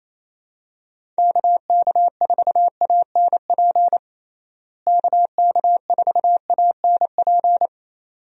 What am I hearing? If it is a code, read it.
Morse code, heard as a keyed tone, about 28 wpm, 700 Hz. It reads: KK4ANP KK4ANP
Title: USA Call Signs - 6 Characters - Repeated 2x - 28wpm